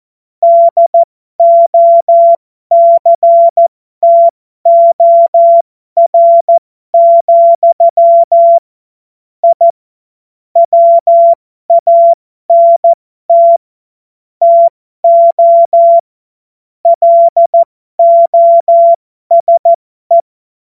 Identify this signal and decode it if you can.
Morse code, heard as a keyed tone, about 14 wpm, spaced as for 12 wpm, 680 Hz. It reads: DOCTOR, I WANT TO LOSE
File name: Morsekurs / Witze